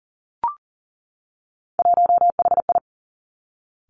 Morse code, heard as a keyed tone, about 38 wpm, 690 Hz. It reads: 1HI
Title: Three Challenging Letter-Number Combinations 40wpm